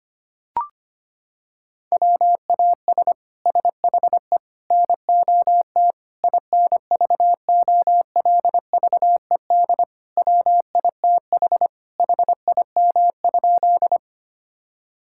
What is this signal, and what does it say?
Morse code, heard as a keyed tone, about 25 wpm, 700 Hz. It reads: WAS SHE NOT INVOLVED WITH HIM?
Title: Sentences from Top 1000 Words - Review Part 5 - 25wpm